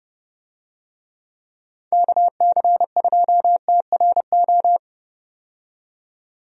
Morse code, heard as a keyed tone, about 32 wpm, 700 Hz. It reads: KC2TRO
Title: USA Call Signs - 6 Characters - 30wpm